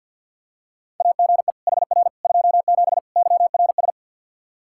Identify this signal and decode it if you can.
Morse code, heard as a keyed tone, about 51 wpm, 700 Hz. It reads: AGE HR 27 YRS
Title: Practice QSO - Exchange 3 - More Info - 50wpm